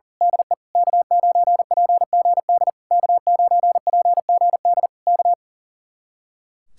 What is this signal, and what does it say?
Morse code, heard as a keyed tone, about 35 wpm, 700 Hz. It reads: DE K9PGD K9PGD K
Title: Speed-Racing Common QSO Phrases - 40wpm